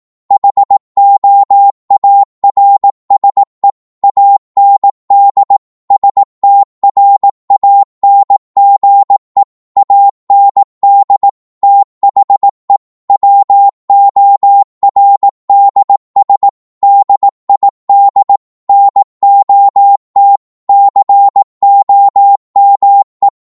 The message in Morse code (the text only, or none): HOARSE AND STRANGE AND THE WORDS DID NOT COME